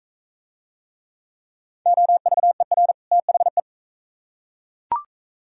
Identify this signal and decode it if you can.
Morse code, heard as a keyed tone, about 40 wpm, 700 Hz. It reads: OVER THE